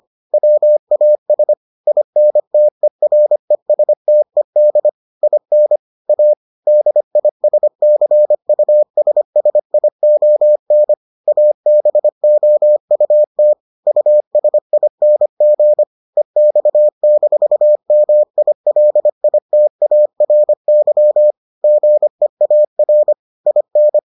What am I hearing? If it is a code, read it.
Morse code, heard as a keyed tone, about 25 wpm, 600 Hz. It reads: WAS INTERESTED IN A DISCUSSION ABOUT USING EX-MILITARY GEAR IN